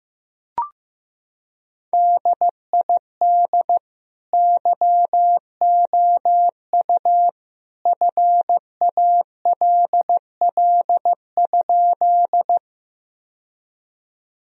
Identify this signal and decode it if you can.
Morse code, heard as a keyed tone, about 15 wpm, 700 Hz. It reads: DID YOU FALL?
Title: Sentences from Top 1000 Words - Review Part 2 - 15wpm